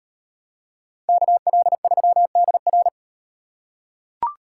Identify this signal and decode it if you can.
Morse code, heard as a keyed tone, about 38 wpm, 700 Hz. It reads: KL3DR